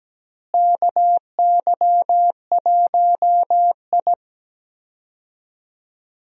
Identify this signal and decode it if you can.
Morse code, heard as a keyed tone, about 17 wpm, 700 Hz. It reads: KY1I
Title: USA Call Signs - 4 Characters - Repeated 2x - 17wpm